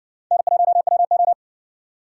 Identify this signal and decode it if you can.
Morse code, heard as a keyed tone, about 56 wpm, 700 Hz. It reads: N1WO